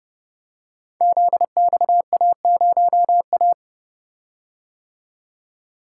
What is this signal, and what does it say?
Morse code, heard as a keyed tone, about 30 wpm, 700 Hz. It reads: ZXA0A
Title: Code Groups - 5 Letter:Numbers - 30wpm